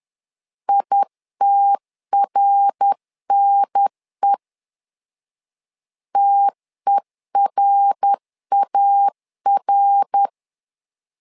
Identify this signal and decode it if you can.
Morse code, heard as a keyed tone, about 10 wpm, 800 Hz. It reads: ITRNE TERAR
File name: LEZIONI / 3 MWRO / 3 MWR 10wpm 1952